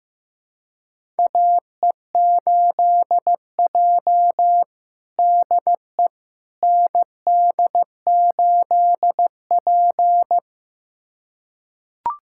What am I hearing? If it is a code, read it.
Morse code, heard as a keyed tone, about 16 wpm, 700 Hz. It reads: AE8J DE ND8P
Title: USA Callsign DE Callsign - 4 Characters - 15wpm